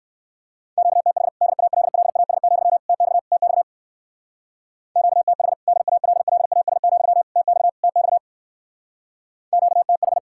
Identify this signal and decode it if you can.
Morse code, heard as a keyed tone, about 54 wpm, 700 Hz. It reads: QTH DALLAS, TX TX QTH DALLAS, TX TX QTH
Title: Speed-Racing Common QSO Phrases - 45wpm